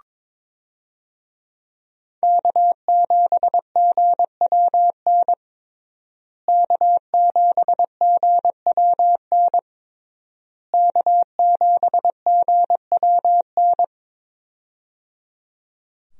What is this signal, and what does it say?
Morse code, heard as a keyed tone, about 22 wpm, 700 Hz. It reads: K7GWN K7GWN K7GWN
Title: USA Call Signs - 5 Characters - Repeated 3x - 22wpm